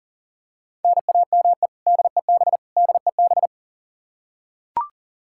Morse code, heard as a keyed tone, about 40 wpm, 700 Hz. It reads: NAME DEB DEB
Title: Practice QSO - Exchange 1 - Beginning - 40wpm